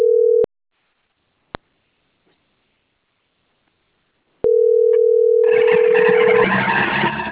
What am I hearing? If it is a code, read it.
Ringback tone (North America), 440 + 480 Hz.